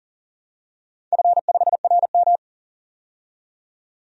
Morse code, heard as a keyed tone, about 40 wpm, 700 Hz. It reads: F5RM
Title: DXCC-100 Call Signs - 4 Characters - 40wpm